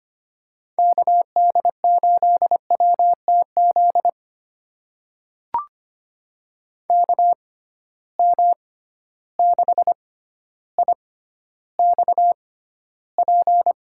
Morse code, heard as a keyed tone, about 25 wpm, spaced as for 13 wpm, 700 Hz. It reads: KD8WTZ KM6IXP
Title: USA Call Signs - 6 Characters - 6x Character Spacing - 25wpm